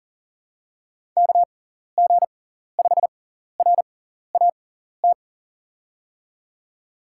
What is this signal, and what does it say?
Morse code, heard as a keyed tone, about 40 wpm, spaced as for 14 wpm, 700 Hz. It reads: KG5RAT
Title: USA Call Signs - 6 Characters - 6x Character Spacing - 40wpm